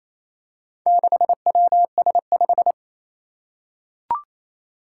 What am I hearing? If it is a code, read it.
Morse code, heard as a keyed tone, about 27 wpm, 700 Hz. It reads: 6WS5
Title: Four Challenging Letter-Number Combinations 28wpm